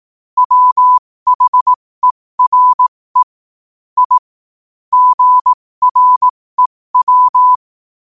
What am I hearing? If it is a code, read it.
Morse code, heard as a keyed tone, about 18 wpm, spaced as for 15 wpm, 1000 Hz. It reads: WHERE I GREW
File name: qotdfiles / 15wpm